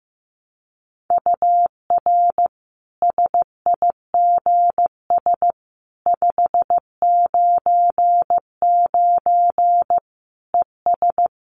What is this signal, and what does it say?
Morse code, heard as a keyed tone, about 15 wpm, 700 Hz. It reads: UR SIGS 599 ES
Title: Common QSO Phrases 15wpm